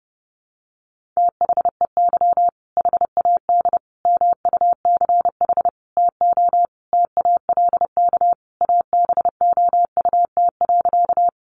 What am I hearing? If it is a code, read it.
Morse code, heard as a keyed tone, about 30 wpm, 700 Hz. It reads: THEY HAD MUCH TO TALK ABOUT.
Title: Sentences from Top 400 Words 30wpm